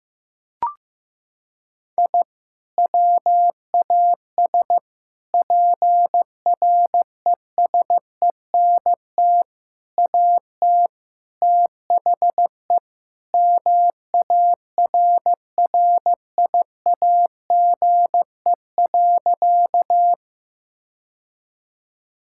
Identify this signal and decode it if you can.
Morse code, heard as a keyed tone, about 16 wpm, 700 Hz. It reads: I WAS PRESENT AT THE MARRIAGE.
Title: Sentences from Top 1000 Words - Review Part 5 - 15wpm